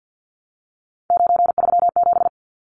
Morse code, heard as a keyed tone, about 50 wpm, 700 Hz. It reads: 937
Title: Numbers - 3-Digit 50wpm